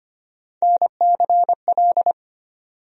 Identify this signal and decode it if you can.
Morse code, heard as a keyed tone, about 25 wpm, 700 Hz. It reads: NCL
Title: USA Call Sign Postfixes - 3 Letters 25wpm